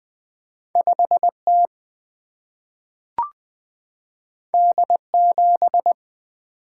Morse code, heard as a keyed tone, about 20 wpm, 700 Hz. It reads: 5 T D 7
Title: Single-Letter Single-Number 20wpm